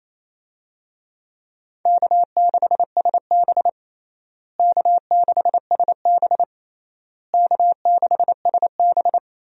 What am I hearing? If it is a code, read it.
Morse code, heard as a keyed tone, about 27 wpm, 700 Hz. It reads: K6SB K6SB K6SB
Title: USA Call Signs - 4 Characters - Repeated 3x - 28wpm